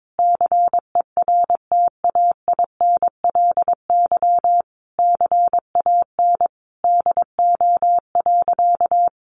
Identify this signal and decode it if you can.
Morse code, heard as a keyed tone, about 22 wpm, 700 Hz. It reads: CERTAINLY CAN DO.
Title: Sentences from Top 900 Words 22wpm